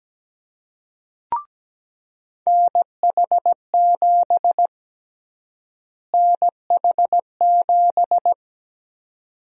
Morse code, heard as a keyed tone, about 17 wpm, 700 Hz. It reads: NH7 NH7
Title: USA Call Sign Prefixes - Repeated 2x - 17wpm